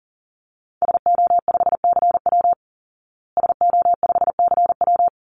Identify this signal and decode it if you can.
Morse code, heard as a keyed tone, about 40 wpm, 700 Hz. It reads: SO5CW SO5CW
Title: Call Signs Repeated 3x - 40wpm